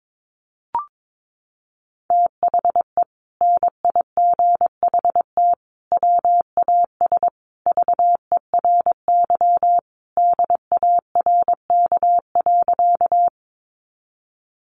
Morse code, heard as a keyed tone, about 22 wpm, 700 Hz. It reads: THE NIGHT WAS VERY DARK.
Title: Sentences from Top 500 Words 22wpm